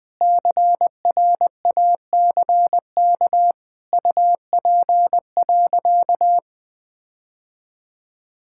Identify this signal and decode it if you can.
Morse code, heard as a keyed tone, about 20 wpm, 700 Hz. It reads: CRACK UP.
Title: All-Age Appropriate Jokes - 20wpm